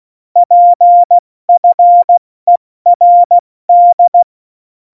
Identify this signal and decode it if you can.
Morse code, heard as a keyed tone, about 16 wpm, 700 Hz. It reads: PFERD
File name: Morsekurs / Witze / Witz 26 Mai, 2020 70 F80